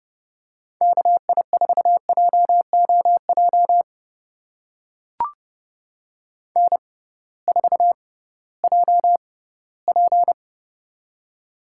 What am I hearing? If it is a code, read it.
Morse code, heard as a keyed tone, about 31 wpm, spaced as for 18 wpm, 700 Hz. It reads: KI4JOJ N4JP
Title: USA Call Signs - 6x Character Spacing - 30wpm